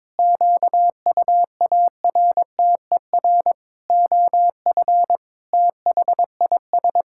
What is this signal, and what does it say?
Morse code, heard as a keyed tone, about 22 wpm, 700 Hz. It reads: QUARTER OF THIS
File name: practice / 22wpm / Sets of 7 Words - 22wpm